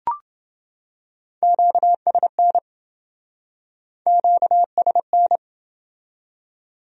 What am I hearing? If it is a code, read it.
Morse code, heard as a keyed tone, about 27 wpm, 700 Hz. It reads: QSN QSN